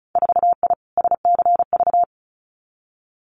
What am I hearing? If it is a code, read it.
Morse code, heard as a keyed tone, about 36 wpm, 700 Hz. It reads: 4I SCV
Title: Winter Field Day Exchanges Encore 35wpm